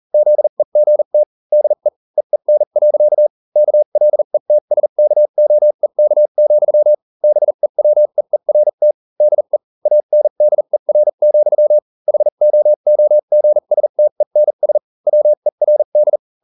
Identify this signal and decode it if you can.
Morse code, heard as a keyed tone, about 40 wpm, spaced as for 35 wpm, 600 Hz. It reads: ZEGT DE EEN. KLETSKOEK, BEWEERT DE ANDER, HOOGSTENS WERD